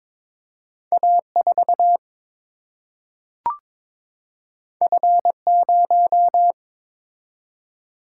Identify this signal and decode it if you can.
Morse code, heard as a keyed tone, about 22 wpm, 700 Hz. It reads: A 4 F 0
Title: Single-Letter Single-Number 22wpm